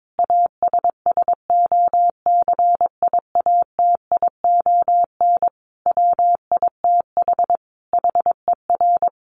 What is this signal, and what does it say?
Morse code, heard as a keyed tone, about 22 wpm, 700 Hz. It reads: ASSOCIATION WITH HER
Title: Sentences from Top 1100 Words 22wpm